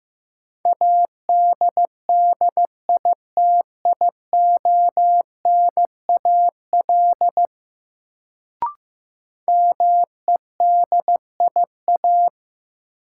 Morse code, heard as a keyed tone, about 15 wpm, 700 Hz. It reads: ADDITIONAL MEDIA
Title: Sentences from Top 1000 Words - Review Part 4 - New Word Drill - 15wpm